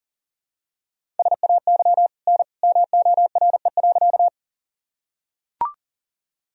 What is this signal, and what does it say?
Morse code, heard as a keyed tone, about 40 wpm, 700 Hz. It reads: SAY N MORE.
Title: Sentences from Top 200 Words and Abbreviations 40wpm